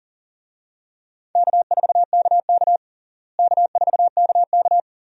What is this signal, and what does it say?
Morse code, heard as a keyed tone, about 40 wpm, 700 Hz. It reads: K4KK K4KK
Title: USA Call Signs - 4 Characters - Repeated 3x - 40wpm